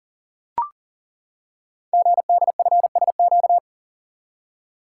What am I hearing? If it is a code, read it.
Morse code, heard as a keyed tone, about 39 wpm, 700 Hz. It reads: GDFSQ
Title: Code Groups - 5 Letters - 40wpm